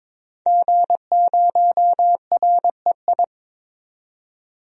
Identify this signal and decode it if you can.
Morse code, heard as a keyed tone, about 22 wpm, 700 Hz. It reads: G0REI
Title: DXCC-100 Call Signs - 5 Characters - 22wpm